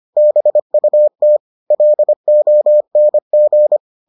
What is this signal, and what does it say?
Morse code, heard as a keyed tone, about 25 wpm, 600 Hz. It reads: BUT LONG